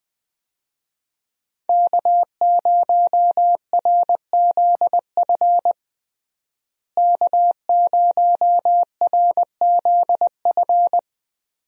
Morse code, heard as a keyed tone, about 20 wpm, 700 Hz. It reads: K0RZF K0RZF
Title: USA Call Signs - 5 Characters - Repeated 2x - 20wpm